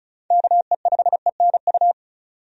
Morse code, heard as a keyed tone, about 36 wpm, 700 Hz. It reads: KE5ENU
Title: USA Call Signs - 6 Characters - 35wpm